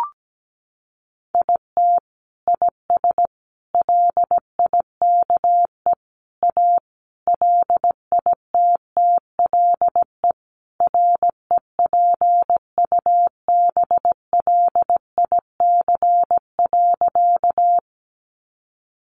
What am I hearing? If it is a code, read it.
Morse code, heard as a keyed tone, about 17 wpm, 700 Hz. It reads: IT IS LIKE A LITTLE REPUBLIC.
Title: Sentences from Top 1500 Words 17wpm